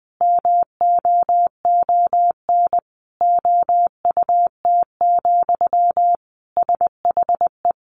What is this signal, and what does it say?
Morse code, heard as a keyed tone, about 20 wpm, 700 Hz. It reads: MOON OUT, SHE
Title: Sentences from Top 500 Words 20wpm